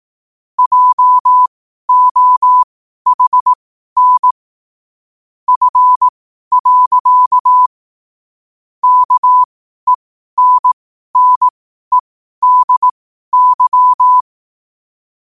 Morse code, heard as a keyed tone, about 18 wpm, spaced as for 12 wpm, 1000 Hz. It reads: JOHN F. KENNEDY
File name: qotdfiles / 12wpm